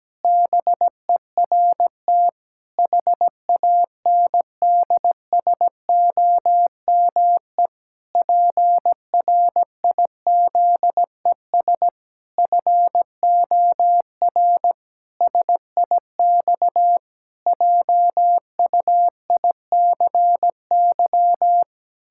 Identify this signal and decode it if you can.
Morse code, heard as a keyed tone, about 17 wpm, 700 Hz. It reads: BERT HANDSOME PRIZES FOR SIX JUICY
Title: Pangram Challenge v2 - 17wpm